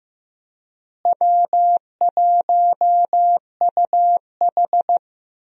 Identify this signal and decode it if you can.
Morse code, heard as a keyed tone, about 15 wpm, 700 Hz. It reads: W1UH
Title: USA Call Signs - 4 Characters - Repeated 2x - 15wpm